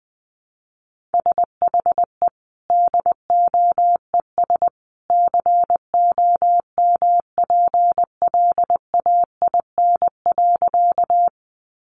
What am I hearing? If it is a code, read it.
Morse code, heard as a keyed tone, about 20 wpm, 700 Hz. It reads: SHE DOES COMPLAIN.
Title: Sentences from Top 1300 Words 20wpm